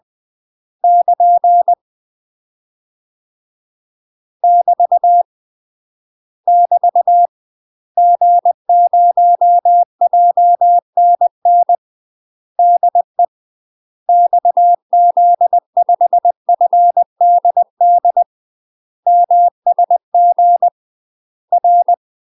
Morse code, heard as a keyed tone, about 20 wpm, 700 Hz. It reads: <KN> <BT> <BT> G0JNN DE XZ5FDD MSG R
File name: UK MASTER QSOs TRAINING / 20WPM / UK MASTER QSOs 0012